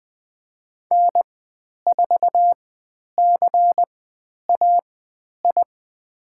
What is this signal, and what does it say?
Morse code, heard as a keyed tone, about 20 wpm, spaced as for 10 wpm, 700 Hz. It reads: N4CAI
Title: 102 Lesson 17g - Letter u - Call Signs 20wpm